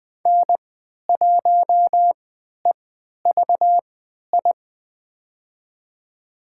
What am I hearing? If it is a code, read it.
Morse code, heard as a keyed tone, about 20 wpm, spaced as for 11 wpm, 700 Hz. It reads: N1EVI
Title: USA Call Signs - 5 Characters - 3x Character Spacing - 20wpm